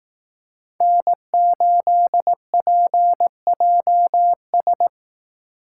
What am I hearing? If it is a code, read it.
Morse code, heard as a keyed tone, about 18 wpm, 700 Hz. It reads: N8PJS
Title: Speed Racing - USA 1x3 Call Signs 15wpm